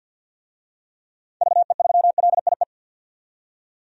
Morse code, heard as a keyed tone, about 49 wpm, 700 Hz. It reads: VE3LIE
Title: DXCC-100 Call Signs - 6 Characters - 50wpm